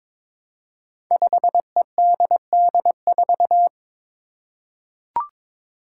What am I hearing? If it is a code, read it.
Morse code, heard as a keyed tone, about 22 wpm, 700 Hz. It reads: 5EDD4
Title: Five Challenging Letter-Number Combinations 22wpm